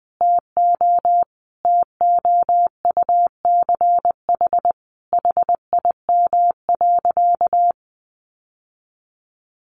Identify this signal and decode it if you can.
Morse code, heard as a keyed tone, about 20 wpm, 700 Hz. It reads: TO TOUCH HIM.
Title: Sentences from Top 1200 Words 20wpm